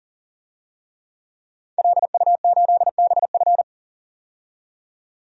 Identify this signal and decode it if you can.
Morse code, heard as a keyed tone, about 38 wpm, 700 Hz. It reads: LU8BF
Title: DXCC-100 Call Signs - 5 Characters - 40wpm